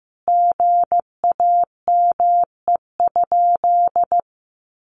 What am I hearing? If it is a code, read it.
Morse code, heard as a keyed tone, about 15 wpm, 700 Hz. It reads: GAME?
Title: Sentences from Top 1300 Words 15wpm